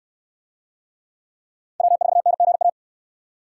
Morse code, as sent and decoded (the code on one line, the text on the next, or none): .-- ....- -. --. .-
W4NGA